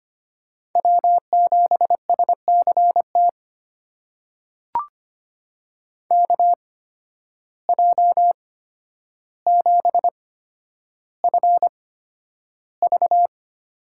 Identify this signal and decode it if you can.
Morse code, heard as a keyed tone, about 25 wpm, spaced as for 11 wpm, 700 Hz. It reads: W7SCT KJ7FV
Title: USA Call Signs - 8x Character Spacing - 25wpm